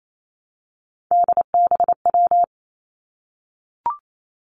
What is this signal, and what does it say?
Morse code, heard as a keyed tone, about 28 wpm, 700 Hz. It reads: DBW
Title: Three Challenging Letter-Number Combinations 28wpm